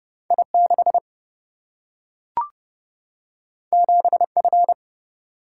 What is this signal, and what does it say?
Morse code, heard as a keyed tone, about 29 wpm, 700 Hz. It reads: I6 7F
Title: Single-Letter Single-Number 30wpm